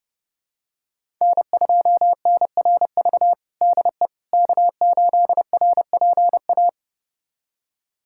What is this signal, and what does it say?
Morse code, heard as a keyed tone, about 30 wpm, 700 Hz. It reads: N2NRV DE K8RPA
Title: USA Callsign DE Callsign - 5 Characters - 30wpm